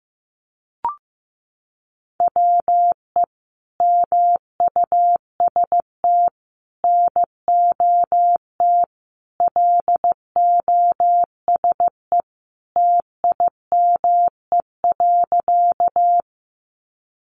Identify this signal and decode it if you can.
Morse code, heard as a keyed tone, about 14 wpm, 700 Hz. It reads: WE MUST NOT LOSE TIME.
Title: Sentences from Top 1100 Words 15wpm